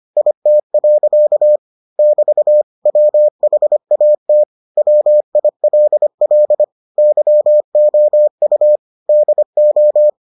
Morse code, heard as a keyed tone, about 25 wpm, 600 Hz. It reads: IT. <BT> WHAT WILL YOU DO